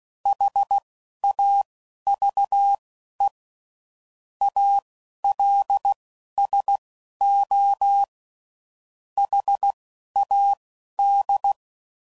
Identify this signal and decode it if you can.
Morse code, heard as a keyed tone, about 16 wpm, spaced as for 11 wpm, 800 Hz. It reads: HAVE ALSO HAD
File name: CW Code Groups @10WPM / Weather Sets / Weather 5 5